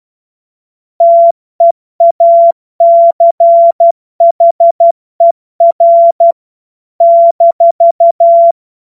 Morse code, heard as a keyed tone, about 12 wpm, 680 Hz. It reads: TEACHER -
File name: Morsekurs / Witze